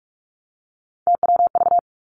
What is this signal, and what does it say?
Morse code, heard as a keyed tone, about 46 wpm, 700 Hz. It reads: TWV
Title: Three Challenging Letter-Number Combinations 45wpm